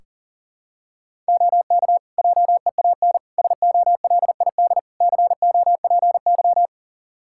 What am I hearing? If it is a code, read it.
Morse code, heard as a keyed tone, about 40 wpm, 700 Hz. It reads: OK JEAN SOLID COPY
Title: Practice QSO - Exchange 3 - More Info - 40wpm